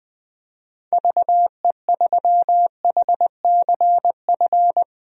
Morse code, heard as a keyed tone, about 20 wpm, 700 Hz. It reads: VE3HCF
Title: DXCC-100 Call Signs - 6 Characters - Repeated 3x - 20wpm